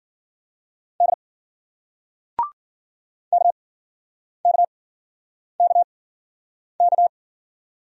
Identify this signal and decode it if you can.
Morse code, heard as a keyed tone, about 51 wpm, spaced as for 9 wpm, 700 Hz. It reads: D KKKK